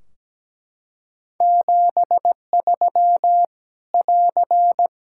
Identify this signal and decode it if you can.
Morse code, heard as a keyed tone, about 17 wpm, 700 Hz. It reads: 73 <AR>
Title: Practice QSO - Exchange 4 - Ending - 17wpm